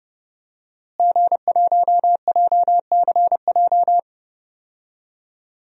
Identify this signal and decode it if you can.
Morse code, heard as a keyed tone, about 30 wpm, 700 Hz. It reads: G1JCJ
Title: DXCC-100 Call Signs - 5 Characters - Repeated 3x - 30wpm